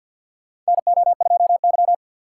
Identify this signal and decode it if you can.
Morse code, heard as a keyed tone, about 50 wpm, 700 Hz. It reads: NO2Y
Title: USA Call Signs - 6x Character Spacing - 50wpm